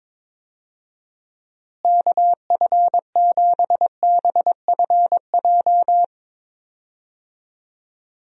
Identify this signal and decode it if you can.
Morse code, heard as a keyed tone, about 21 wpm, 700 Hz. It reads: KF7BFJ
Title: USA Call Signs - 6 Characters - Repeated 2x - 22wpm